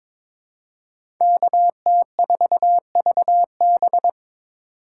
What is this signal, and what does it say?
Morse code, heard as a keyed tone, about 22 wpm, 700 Hz. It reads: KT4VB
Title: USA Call Signs - 5 Characters - 2x Character Spacing - 22wpm